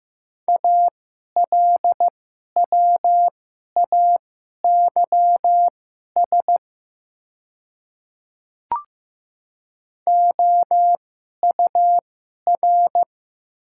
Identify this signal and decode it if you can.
Morse code, heard as a keyed tone, about 15 wpm, spaced as for 11 wpm, 690 Hz. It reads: ALWAYS OUR
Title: 2x Character Spacing - Top 300 Words - 15wpm